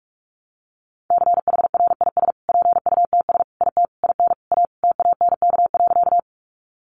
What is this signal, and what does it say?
Morse code, heard as a keyed tone, about 44 wpm, 700 Hz. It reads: CHRIS PUTS IT IN A TANK.
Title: Easy Phrases 45wpm